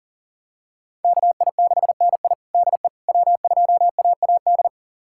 Morse code, heard as a keyed tone, about 40 wpm, 700 Hz. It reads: KI6NI DE W2AAD
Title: USA Callsign DE Callsign - 5 Characters - 40wpm